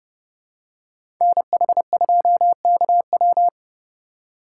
Morse code, heard as a keyed tone, about 30 wpm, 700 Hz. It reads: NH2KW
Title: USA Call Signs - 5 Characters - 3x Character Spacing - 30wpm